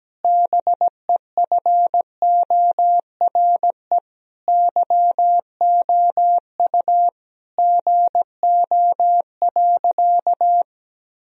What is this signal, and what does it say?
Morse code, heard as a keyed tone, about 17 wpm, 700 Hz. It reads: BEFORE YOU GO.
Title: Sentences from Top 1000 Words - Review Part 1 - 17wpm